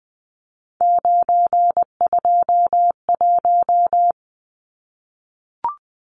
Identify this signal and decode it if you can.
Morse code, heard as a keyed tone, about 20 wpm, 700 Hz. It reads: 921
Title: Numbers - 3-Digit 20wpm